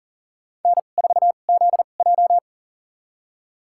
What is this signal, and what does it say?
Morse code, heard as a keyed tone, about 40 wpm, spaced as for 29 wpm, 700 Hz. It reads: N4ZJ
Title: USA Call Signs - 2x Character Spacing - 40wpm